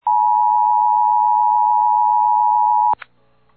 Emergency Alert System (SAME) tones (the 853 and 960 Hz attention tone). It is heard from 0.0 to 3.0 s.
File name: eas / KWBE / KWBE